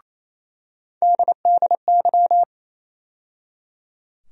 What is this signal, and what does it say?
Morse code, heard as a keyed tone, about 28 wpm, 700 Hz. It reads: DDY
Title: USA Call Sign Postfixes - 3 Letters 28wpm